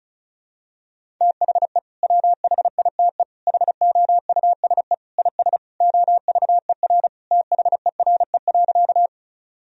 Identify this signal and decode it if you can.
Morse code, heard as a keyed tone, about 36 wpm, 700 Hz. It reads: THE WHITE HOUSE IS OVER THERE.
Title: Sentences from Top 1000 Words - Review Part 1 - 35wpm